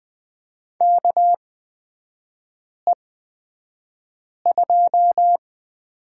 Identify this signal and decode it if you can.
Morse code, heard as a keyed tone, about 20 wpm, spaced as for 5 wpm, 700 Hz. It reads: KE2
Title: 229 Lesson 37e - Letter x - Three Characters 20wpm